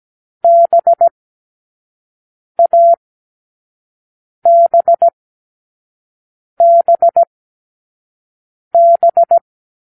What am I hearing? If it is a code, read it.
Morse code, heard as a keyed tone, about 17 wpm, spaced as for 5 wpm, 680 Hz. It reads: BABBB